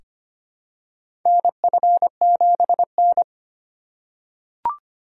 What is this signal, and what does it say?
Morse code, heard as a keyed tone, about 25 wpm, 700 Hz. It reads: NF7N
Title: USA Call Signs - 4 Characters - 3x Character Spacing - 25wpm